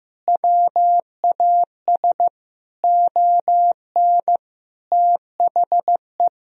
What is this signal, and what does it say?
Morse code, heard as a keyed tone, about 15 wpm, 700 Hz. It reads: WAS ON THE
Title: Sentences from Top 1000 Words - Review Part 2 - 15wpm